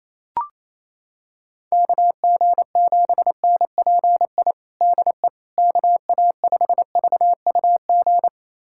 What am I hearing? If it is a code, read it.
Morse code, heard as a keyed tone, about 28 wpm, 700 Hz. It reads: KG7NPI DE KA5VUG
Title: USA Callsign DE Callsign - 6 Characters - 28wpm